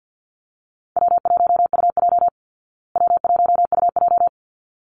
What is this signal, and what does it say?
Morse code, heard as a keyed tone, about 49 wpm, 700 Hz. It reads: W1UJ W1UJ
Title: Call Signs Repeated 3x - 50wpm